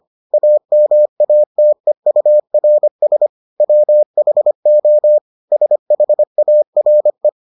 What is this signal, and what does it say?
Morse code, heard as a keyed tone, about 25 wpm, 600 Hz. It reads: AMATEURS WHO SHARE